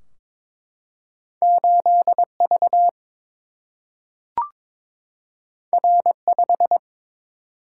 Morse code, heard as a keyed tone, about 22 wpm, 700 Hz. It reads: 8 V R 5
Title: Single-Letter Single-Number 22wpm